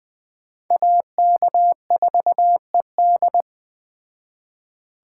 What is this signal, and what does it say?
Morse code, heard as a keyed tone, about 21 wpm, 700 Hz. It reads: AK4ED
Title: Speed Racing - USA 2x2 Call Signs 17wpm